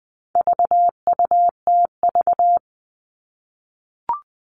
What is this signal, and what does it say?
Morse code, heard as a keyed tone, about 20 wpm, 700 Hz. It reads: VUTV
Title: All Cut Numbers - 4-Digit 20wpm